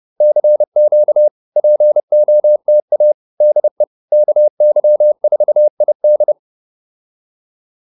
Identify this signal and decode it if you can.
Morse code, heard as a keyed tone, about 30 wpm, 600 Hz. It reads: CQ POTA DE KY4ID